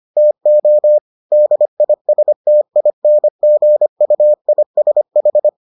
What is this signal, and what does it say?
Morse code, heard as a keyed tone, about 25 wpm, 600 Hz. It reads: TO DISTINGUISH